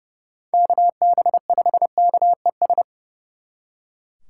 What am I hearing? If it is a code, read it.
Morse code, heard as a keyed tone, about 36 wpm, 700 Hz. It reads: KB5KES
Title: Speed Racing - USA 2x3 Call Signs 30wpm